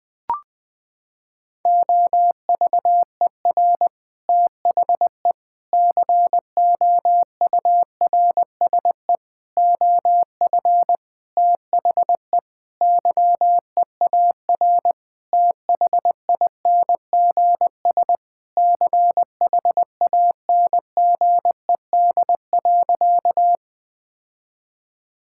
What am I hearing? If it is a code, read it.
Morse code, heard as a keyed tone, about 20 wpm, 700 Hz. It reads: OVER THE COURSE OF THE YEAR THINGS CHANGED.
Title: Sentences with 5-Word Phrases - 20wpm